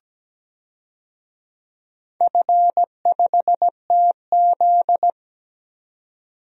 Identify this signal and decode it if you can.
Morse code, heard as a keyed tone, about 17 wpm, 700 Hz. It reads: F5TZ
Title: DXCC-100 Call Signs - 4 Characters - 17wpm